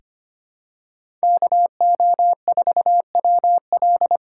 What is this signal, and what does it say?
Morse code, heard as a keyed tone, about 25 wpm, 700 Hz. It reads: KO4WL
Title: USA Call Signs - 5 Characters - 4x Character Spacing - 25wpm